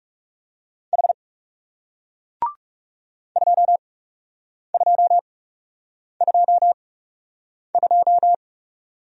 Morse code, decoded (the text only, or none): H 2222